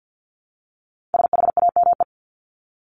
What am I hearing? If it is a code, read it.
Morse code, heard as a keyed tone, about 50 wpm, 700 Hz. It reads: SHAME
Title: Top 1751-2000 Words 50wpm